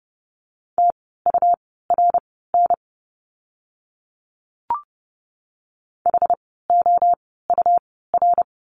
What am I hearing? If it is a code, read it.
Morse code, heard as a keyed tone, about 30 wpm, spaced as for 17 wpm, 700 Hz. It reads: TURN HOUR
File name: Lesson 17f - Letter u - Words 30wpm